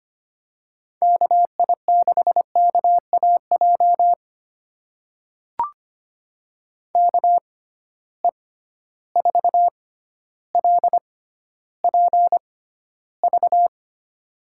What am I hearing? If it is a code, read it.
Morse code, heard as a keyed tone, about 25 wpm, spaced as for 13 wpm, 700 Hz. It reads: KI6KAJ KE4LPV
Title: USA Call Signs - 6 Characters - 6x Character Spacing - 25wpm